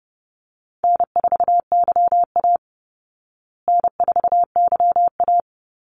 Morse code, heard as a keyed tone, about 30 wpm, 700 Hz. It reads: N4YA N4YA
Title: Easy Call Signs Repeated 3x - 30wpm